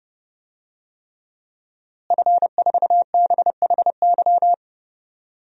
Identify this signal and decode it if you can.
Morse code, heard as a keyed tone, about 30 wpm, 700 Hz. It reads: F4BHY
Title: DXCC-100 Call Signs - 5 Characters - Repeated 3x - 30wpm